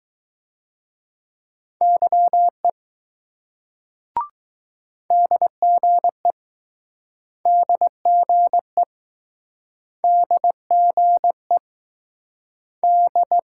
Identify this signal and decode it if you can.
Morse code, heard as a keyed tone, about 20 wpm, 700 Hz. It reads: YE DGE DGE DGE D